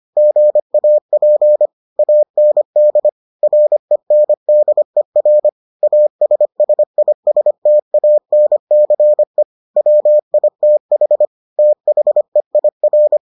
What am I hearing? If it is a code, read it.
Morse code, heard as a keyed tone, about 25 wpm, 600 Hz. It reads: GAP AND RENDER ASSISTANCE WITH THEIR